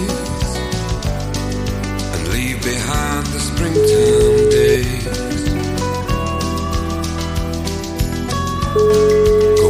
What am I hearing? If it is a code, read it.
Ringback tone (Europe, 425 Hz), with noise.